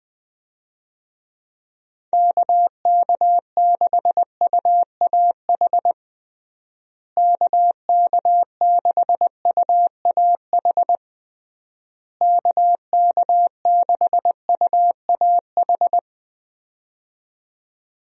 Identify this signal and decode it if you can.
Morse code, heard as a keyed tone, about 20 wpm, 700 Hz. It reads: KK6UAH KK6UAH KK6UAH
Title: USA Call Signs - 6 Characters - Repeated 3x - 20wpm